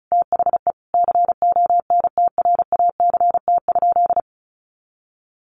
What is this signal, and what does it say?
Morse code, heard as a keyed tone, about 34 wpm, 700 Hz. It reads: THE CONTRACT?
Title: Sentences from Top 1400 Words 35wpm